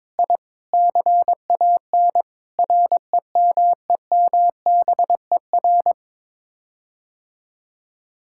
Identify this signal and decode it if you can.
Morse code, heard as a keyed tone, about 21 wpm, 700 Hz. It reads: I CAN REMEMBER
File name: practice / 22wpm / Sets of 7 Words - 22wpm